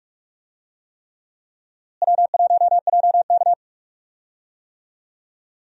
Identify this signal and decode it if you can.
Morse code, heard as a keyed tone, about 46 wpm, 700 Hz. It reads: W1JK
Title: USA Call Signs - 4 Characters - Repeated 3x - 45wpm